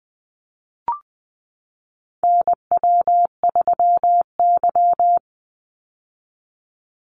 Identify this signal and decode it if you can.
Morse code, heard as a keyed tone, about 20 wpm, 700 Hz. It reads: NW3Y
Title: Call Signs - 20wpm